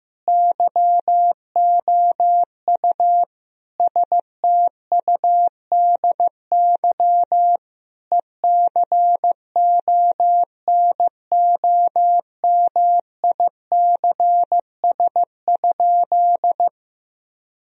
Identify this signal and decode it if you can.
Morse code, heard as a keyed tone, about 15 wpm, 700 Hz. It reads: YOU STUDY ECONOMICS?
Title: Sentences from Top 1000 Words - Review Part 3 - 15wpm